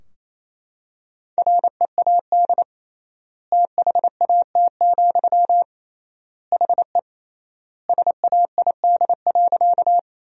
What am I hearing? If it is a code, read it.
Morse code, heard as a keyed tone, about 28 wpm, 700 Hz. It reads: READ THAT, HE SAID.
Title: 3x Word-Spacing - Sentences from Top 200 Words - 28wpm